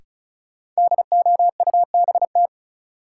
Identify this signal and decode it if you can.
Morse code, heard as a keyed tone, about 34 wpm, 700 Hz. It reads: DOUBT
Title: Sets of 5 Words - New Word Drill - 35wpm